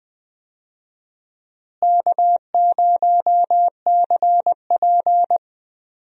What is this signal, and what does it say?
Morse code, heard as a keyed tone, about 21 wpm, 690 Hz. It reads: K0CP
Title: USA Call Signs - 4 Characters - Repeated 3x - 20wpm